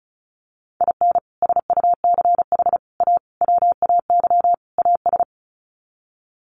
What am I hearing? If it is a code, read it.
Morse code, heard as a keyed tone, about 34 wpm, 700 Hz. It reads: IN SUCH A WAY AS
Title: Sets of 5 Words Encore 35wpm